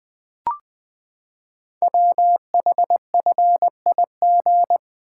Morse code, heard as a keyed tone, about 20 wpm, 700 Hz. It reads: WHFIG
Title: Code Groups - 5 Letters - 20wpm